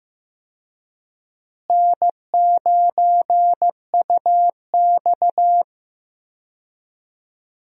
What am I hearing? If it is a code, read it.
Morse code, heard as a keyed tone, about 15 wpm, 700 Hz. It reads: N9UX
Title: USA Call Signs - 4 Characters - Repeated 3x - 15wpm